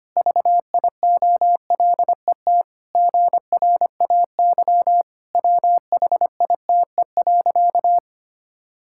Morse code, heard as a keyed tone, about 25 wpm, 700 Hz. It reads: VIOLET GRAY WHITE.
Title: Pangram Challenge v2 - 25wpm